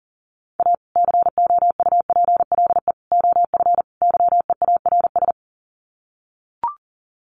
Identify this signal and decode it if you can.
Morse code, heard as a keyed tone, about 40 wpm, 700 Hz. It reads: A COUPLE OF YEARS
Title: Sets of 4 Words 40wpm